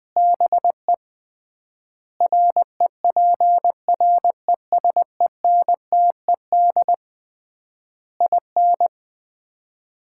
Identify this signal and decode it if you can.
Morse code, heard as a keyed tone, about 20 wpm, 700 Hz. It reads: BE REPRESENTED IN
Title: 3x Word-Spacing - Sets of 5 Words - 20wpm